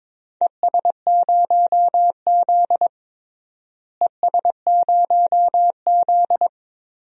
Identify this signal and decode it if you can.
Morse code, heard as a keyed tone, about 21 wpm, 700 Hz. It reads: ES0Z ES0Z
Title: DXCC-100 Call Signs - 4 Characters - Repeated 2x - 22wpm